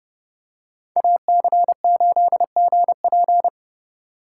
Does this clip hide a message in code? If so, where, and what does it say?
Yes: Morse code from 1.0 to 3.5 s, heard as a keyed tone, about 30 wpm, 700 Hz. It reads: AC8GP